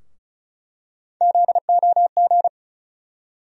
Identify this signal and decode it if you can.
Morse code, heard as a keyed tone, about 34 wpm, 700 Hz. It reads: ZOG
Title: USA Call Sign Postfixes - 3 Letters 35wpm